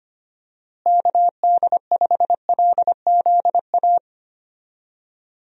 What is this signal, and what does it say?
Morse code, heard as a keyed tone, about 25 wpm, 700 Hz. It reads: KD5LZA
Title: USA Call Signs - 6 Characters - Repeated 3x - 25wpm